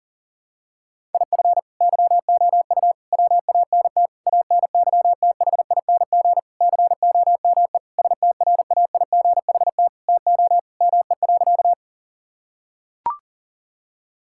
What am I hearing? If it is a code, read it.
Morse code, heard as a keyed tone, about 40 wpm, 700 Hz. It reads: IF YOU WANT ANYTHING COME STRAIGHT TO ME.
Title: Sentences from Top 1000 Words - Review Part 6 - 40wpm